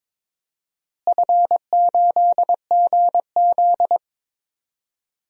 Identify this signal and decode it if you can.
Morse code, heard as a keyed tone, about 22 wpm, 700 Hz. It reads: F8GZ
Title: DXCC-100 Call Signs - 4 Characters - Repeated 2x - 22wpm